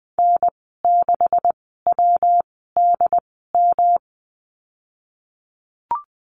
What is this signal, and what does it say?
Morse code, heard as a keyed tone, about 20 wpm, spaced as for 15 wpm, 700 Hz. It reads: N6WDM
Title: 147 Lesson 24g - Letter y - Call Signs 20wpm